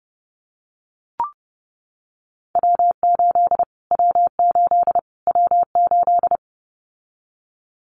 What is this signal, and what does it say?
Morse code, heard as a keyed tone, about 30 wpm, 700 Hz. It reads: W8 W8 W8
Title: Callsign Prefixes for Top 25 DXCC - Repeated 3x - 30wpm